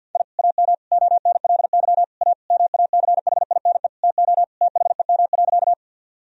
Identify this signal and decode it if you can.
Morse code, heard as a keyed tone, about 51 wpm, 700 Hz. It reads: I AM ONLY A MACHINE TO THEM.
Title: Sentences from Top 1000 Words - Review Part 2 - 50wpm